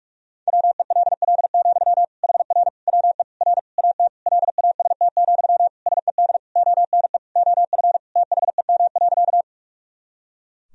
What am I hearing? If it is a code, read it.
Morse code, heard as a keyed tone, about 46 wpm, 700 Hz. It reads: WELL, HR WE R AT LAST, SED ONE OF THEM.
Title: Sentences from Top 300 Words and Abbreviations 45wpm